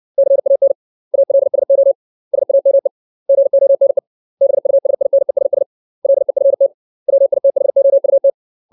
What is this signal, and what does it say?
Morse code, heard as a keyed tone, about 60 wpm, 550 Hz. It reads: CAN ALSO SAME GONE BUSINESS LEFT WITHOUT
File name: 632 words / 632 Words 60WPM